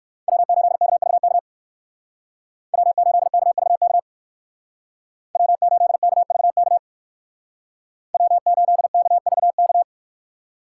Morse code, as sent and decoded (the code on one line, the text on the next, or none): .-- ---.. -.- ...- -.- / .-- ---.. -.- ...- -.- / .-- ---.. -.- ...- -.- / .-- ---.. -.- ...- -.-
W8KVK W8KVK W8KVK W8KVK